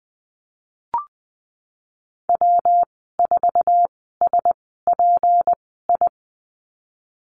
Morse code, heard as a keyed tone, about 20 wpm, spaced as for 15 wpm, 700 Hz. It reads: W4SPI
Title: 154 Lesson 25g - Letter p - Call Signs 20wpm